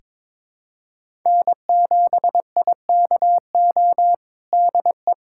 Morse code, heard as a keyed tone, about 22 wpm, 700 Hz. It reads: N7IKO DE
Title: USA Callsign DE Callsign 22wpm